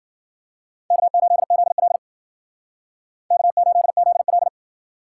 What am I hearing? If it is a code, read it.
Morse code, heard as a keyed tone, about 52 wpm, 700 Hz. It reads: K8ZL K8ZL